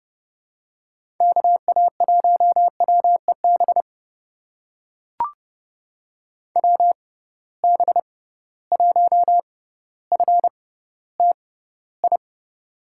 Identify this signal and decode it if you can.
Morse code, heard as a keyed tone, about 32 wpm, spaced as for 16 wpm, 700 Hz. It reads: KA1WEB WB1FTI
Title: USA Call Signs - 6 Characters - 6x Character Spacing - 30wpm